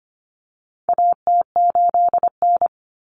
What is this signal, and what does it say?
Morse code, heard as a keyed tone, about 25 wpm, 700 Hz. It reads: AT8N
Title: Common Cut Numbers - 4-Digit 25wpm